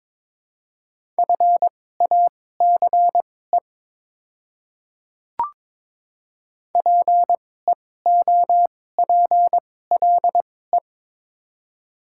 Morse code, heard as a keyed tone, about 21 wpm, spaced as for 16 wpm, 700 Hz. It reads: FACE PEOPLE
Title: 2x Character Spacing - Top 300 Words - 22wpm